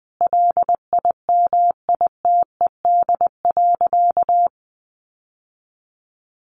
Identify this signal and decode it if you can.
Morse code, heard as a keyed tone, about 20 wpm, 700 Hz. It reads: LIMITED.
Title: Sentences from Top 900 Words 20wpm